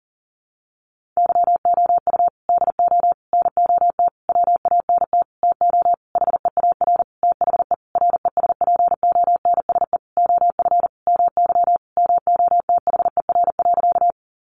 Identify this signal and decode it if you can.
Morse code, heard as a keyed tone, about 40 wpm, 700 Hz. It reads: YOU DO NOT WANT TO HEAR THE RESPONSE OF MY MOTHER.
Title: Sentences from Top 1400 Words 40wpm